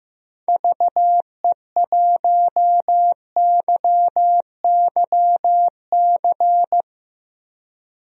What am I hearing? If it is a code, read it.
Morse code, heard as a keyed tone, about 15 wpm, 700 Hz. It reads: VE1YYC
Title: DXCC-100 Call Signs - 6 Characters - 15wpm